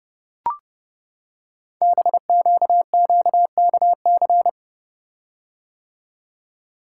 Morse code, heard as a keyed tone, about 30 wpm, 690 Hz. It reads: BQQKC
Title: Code Groups - 5 Letters - 30wpm